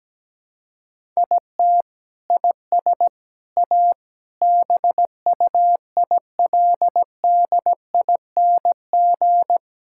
Morse code, heard as a keyed tone, about 17 wpm, 700 Hz. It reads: IT IS A BUILDING
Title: Sentences from Top 1000 Words - Review Part 5 - 17wpm